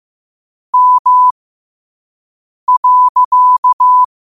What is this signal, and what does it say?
Morse code, heard as a keyed tone, about 15 wpm, spaced as for 5 wpm, 1000 Hz. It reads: M.
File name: qotdfiles / 5wpm